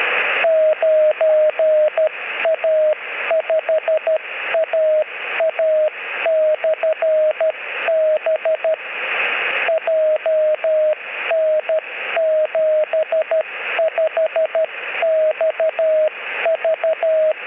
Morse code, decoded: 9A5AA/B JN75XV